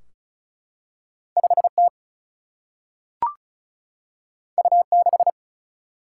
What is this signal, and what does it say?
Morse code, heard as a keyed tone, about 36 wpm, 700 Hz. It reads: 5 T U 6
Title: Single-Letter Single-Number 35wpm